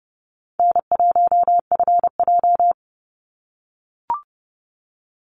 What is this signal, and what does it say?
Morse code, heard as a keyed tone, about 30 wpm, 700 Hz. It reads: N1FJ
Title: NA SOTA Chaser Callsigns 30wpm